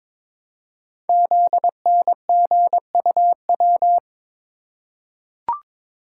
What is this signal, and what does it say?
Morse code, heard as a keyed tone, about 22 wpm, 700 Hz. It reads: ZNGUW
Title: Code Groups - 5 Letters - 22wpm